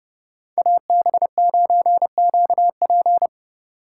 Morse code, heard as a keyed tone, about 31 wpm, 700 Hz. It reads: AB9QP
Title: USA Call Signs - 5 Characters - 2x Character Spacing - 30wpm